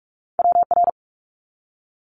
Morse code, heard as a keyed tone, about 46 wpm, 690 Hz. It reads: WR
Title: Multi-letter Phonemes 45wpm